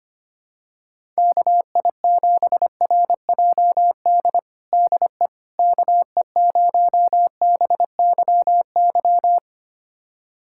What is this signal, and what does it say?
Morse code, heard as a keyed tone, about 25 wpm, 700 Hz. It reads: KI7RJD DE KE0BYY
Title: USA Callsign DE Callsign - 6 Characters - 25wpm